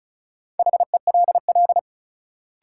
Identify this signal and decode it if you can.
Morse code, heard as a keyed tone, about 34 wpm, 700 Hz. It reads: HELL
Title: Sets of 5 Words Encore - New Word Drill - 35wpm